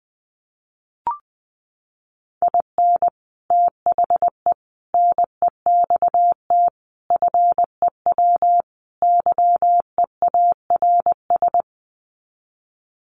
Morse code, heard as a keyed tone, about 20 wpm, 700 Hz. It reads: IN THE NEXT FEW YEARS
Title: Sets of 5 Words Encore 20wpm